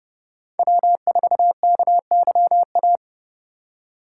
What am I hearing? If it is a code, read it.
Morse code, heard as a keyed tone, about 32 wpm, 700 Hz. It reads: W4KYA
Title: USA Call Signs - 2x Character Spacing - 30wpm